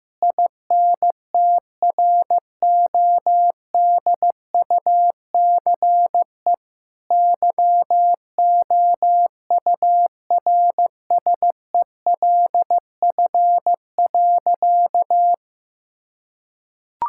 Morse code, heard as a keyed tone, about 15 wpm, 700 Hz. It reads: INTRODUCE YOURSELF.
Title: Sentences from Top 1600 Words 15wpm